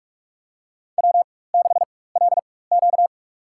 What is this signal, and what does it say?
Morse code, heard as a keyed tone, about 46 wpm, spaced as for 21 wpm, 700 Hz. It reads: W6LQ
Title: USA Call Signs - 4 Characters - 4x Character Spacing - 45wpm